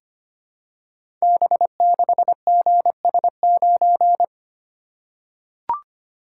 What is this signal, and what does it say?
Morse code, heard as a keyed tone, about 25 wpm, 700 Hz. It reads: B6GS9
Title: Code Groups - 5 Letter:Numbers - 25wpm